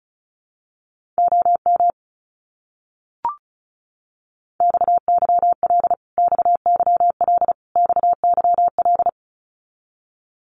Morse code, heard as a keyed tone, about 34 wpm, 700 Hz. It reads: OM XYL XYL XYL
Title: Abbreviations Repeated 3x - 35wpm